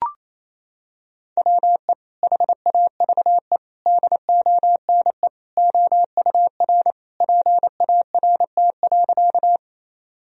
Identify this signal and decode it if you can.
Morse code, heard as a keyed tone, about 28 wpm, 700 Hz. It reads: WE HAVE DONE OUR PART.
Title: Sentences from Top 1000 Words - Review Part 1 - 28wpm